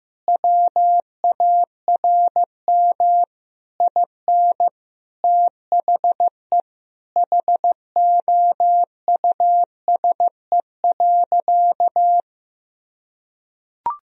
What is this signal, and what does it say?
Morse code, heard as a keyed tone, about 15 wpm, 700 Hz. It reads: WARM IN THE HOUSE.
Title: Sentences from Top 1000 Words - Review Part 4 - 15wpm